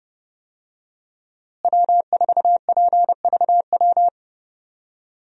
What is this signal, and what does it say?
Morse code, heard as a keyed tone, about 30 wpm, 700 Hz. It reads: W4PVW
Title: USA Call Signs - 5 Characters - Repeated 3x - 30wpm